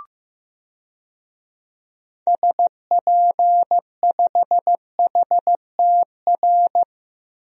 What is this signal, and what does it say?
Morse code, heard as a keyed tone, about 15 wpm, 700 Hz. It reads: SP5HTR
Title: DXCC-100 Call Signs - 6 Characters - 15wpm